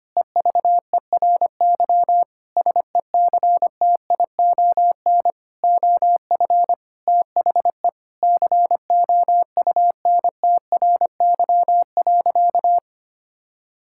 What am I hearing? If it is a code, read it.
Morse code, heard as a keyed tone, about 25 wpm, 700 Hz. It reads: EVERY SECTION OF THE COUNTRY.
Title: Sentences from Top 1000 Words - Review Part 5 - 25wpm